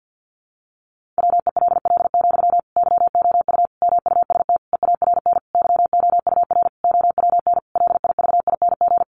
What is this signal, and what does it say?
Morse code, heard as a keyed tone, about 50 wpm, 700 Hz. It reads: WELL, YOU MUST EARN YOUR OWN LIVING
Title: Sentences from Top 1500 Words 50wpm